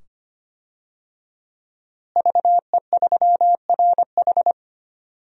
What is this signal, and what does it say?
Morse code, heard as a keyed tone, about 25 wpm, 700 Hz. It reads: VE3RH
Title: DXCC-100 Call Signs - 5 Characters - Repeated 2x - 25wpm